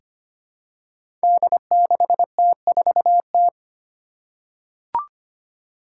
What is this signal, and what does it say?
Morse code, heard as a keyed tone, about 25 wpm, 700 Hz. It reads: D6T4T
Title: Five Challenging Letter-Number Combinations 25wpm